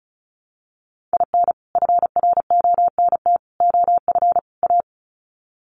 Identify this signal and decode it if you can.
Morse code, heard as a keyed tone, about 34 wpm, 700 Hz. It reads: IN FRONT OF A
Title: Sets of 4 Words 35wpm